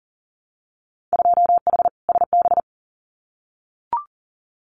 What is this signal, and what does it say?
Morse code, heard as a keyed tone, about 40 wpm, 700 Hz. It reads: 2H SB
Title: Winter Field Day Exchanges Encore 40wpm